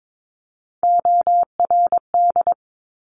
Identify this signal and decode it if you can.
Morse code, heard as a keyed tone, about 22 wpm, 700 Hz. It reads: ORD
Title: Top Three Letter Combinations in Words 22wpm